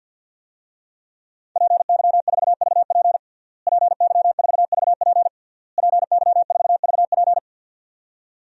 Morse code, heard as a keyed tone, about 49 wpm, 700 Hz. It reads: PY4VP PY4VP PY4VP
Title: DXCC-100 Call Signs - 5 Characters - Repeated 3x - 50wpm